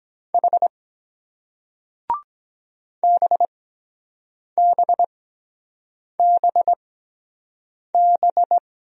Morse code, heard as a keyed tone, about 23 wpm, spaced as for 7 wpm, 700 Hz. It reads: H BBBB